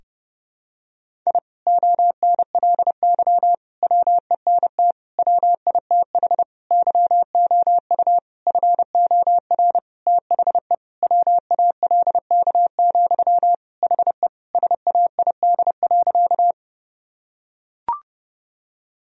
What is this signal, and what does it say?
Morse code, heard as a keyed tone, about 32 wpm, 700 Hz. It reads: I ONLY WENT WITH YOU FOR THE WALK, HE SAID.
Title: Sentences from Top 1000 Words - Review Part 1 - 30wpm